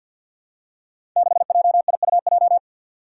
Morse code, heard as a keyed tone, about 48 wpm, 700 Hz. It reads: 6JIUJ
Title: Five Challenging Letter-Number Combinations 50wpm